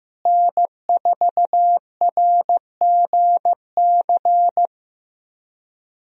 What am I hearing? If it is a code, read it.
Morse code, heard as a keyed tone, about 15 wpm, 700 Hz. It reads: N4RGC
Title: USA Call Signs - 5 Characters - Repeated 3x - 15wpm